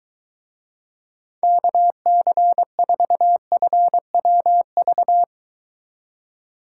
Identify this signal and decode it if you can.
Morse code, heard as a keyed tone, about 23 wpm, 700 Hz. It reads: KC4FWV